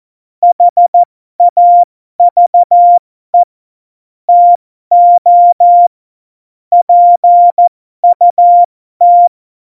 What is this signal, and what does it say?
Morse code, heard as a keyed tone, about 14 wpm, spaced as for 12 wpm, 700 Hz. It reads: HAVE TO PUT
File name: Morsekurs / Witze